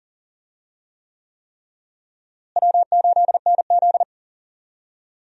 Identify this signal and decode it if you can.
Morse code, heard as a keyed tone, about 40 wpm, 700 Hz. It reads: W8NZ
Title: USA Call Signs - 4 Characters - Repeated 3x - 40wpm